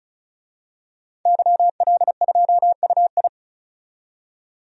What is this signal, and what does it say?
Morse code, heard as a keyed tone, about 36 wpm, 700 Hz. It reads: YL2UI
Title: DXCC-100 Call Signs - 5 Characters - 35wpm